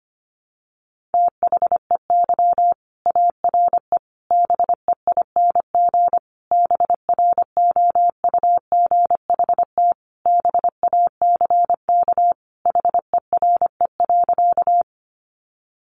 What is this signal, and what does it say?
Morse code, heard as a keyed tone, about 25 wpm, 700 Hz. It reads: THEY ARE BEING BROUGHT BACK HERE.
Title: Sentences from Top 600 Words 25wpm